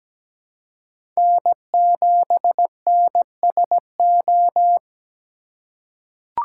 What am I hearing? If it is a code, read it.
Morse code, heard as a keyed tone, about 17 wpm, 700 Hz. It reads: N7NSO
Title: USA Call Signs - 5 Characters - 3x Character Spacing - 17wpm